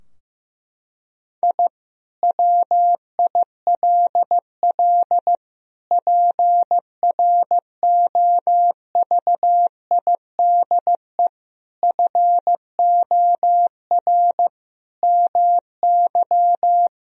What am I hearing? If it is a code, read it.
Morse code, heard as a keyed tone, about 15 wpm, 700 Hz. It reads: I WILL PROVIDE FOR MY
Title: Sentences from Top 1000 Words - Review Part 3 - 15wpm